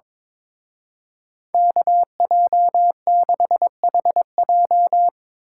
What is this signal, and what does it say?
Morse code, heard as a keyed tone, about 24 wpm, 700 Hz. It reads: KJ6HJ
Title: Speed Racing - USA 2x2 Call Signs 22wpm